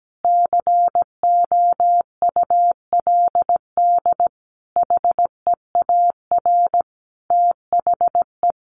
Morse code, heard as a keyed tone, about 17 wpm, 700 Hz. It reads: COULD HEAR THE
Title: Sentences from Top 1100 Words 17wpm